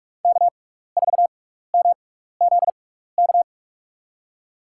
Morse code, heard as a keyed tone, about 47 wpm, spaced as for 16 wpm, 690 Hz. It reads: K4MZK
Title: USA Call Signs - 5 Characters - 6x Character Spacing - 45wpm